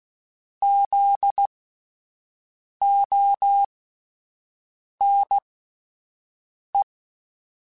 Morse code, heard as a keyed tone, about 16 wpm, spaced as for 5 wpm, 800 Hz. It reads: ZONE
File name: Alphabetical / Alphabetical mp3s / Zs @5